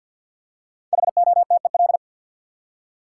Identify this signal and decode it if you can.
Morse code, heard as a keyed tone, about 53 wpm, 700 Hz. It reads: HOTEL
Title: Five Letter Words - 50wpm